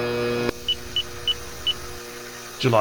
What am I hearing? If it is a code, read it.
There is no code.